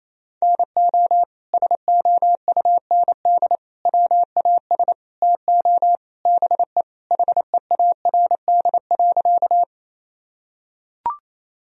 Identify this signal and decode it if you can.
Morse code, heard as a keyed tone, about 28 wpm, 700 Hz. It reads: NO SOUND WAS TO BE HEARD.
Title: Sentences from Top 1000 Words - Review Part 1 - 28wpm